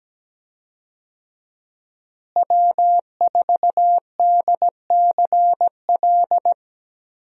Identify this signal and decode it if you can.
Morse code, heard as a keyed tone, about 17 wpm, 700 Hz. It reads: W4DCL
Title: USA Call Signs - 5 Characters - Repeated 2x - 17wpm